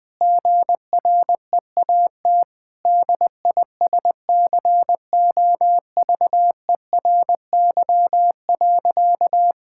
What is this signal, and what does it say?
Morse code, heard as a keyed tone, about 21 wpm, 700 Hz. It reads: GREAT DISCOVERY.
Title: Sentences with 8-Word Phrases - 20wpm